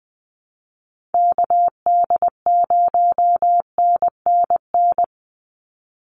Morse code, heard as a keyed tone, about 20 wpm, 700 Hz. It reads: KD0NNN
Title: USA Call Signs 20wpm